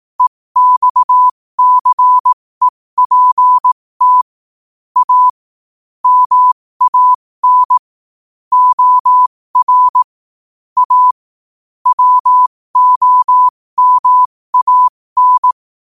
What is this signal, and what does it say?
Morse code, heard as a keyed tone, about 18 wpm, spaced as for 15 wpm, 1000 Hz. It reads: EXCEPT A MAN OR A WOMAN
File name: qotdfiles / 15wpm